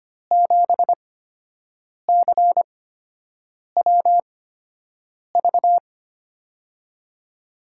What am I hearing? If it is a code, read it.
Morse code, heard as a keyed tone, about 25 wpm, spaced as for 7 wpm, 700 Hz. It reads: 7CWV
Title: USA Call Signs - 6 Characters - 8x Character Spacing - 25wpm